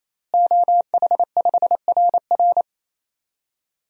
Morse code, heard as a keyed tone, about 28 wpm, 700 Hz. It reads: OH5RR
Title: DXCC-100 Call Signs - 5 Characters - Repeated 3x - 28wpm